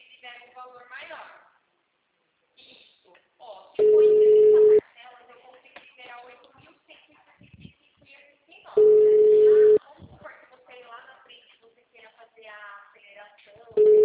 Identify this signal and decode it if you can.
Ringback tone (Europe, 425 Hz), with speech.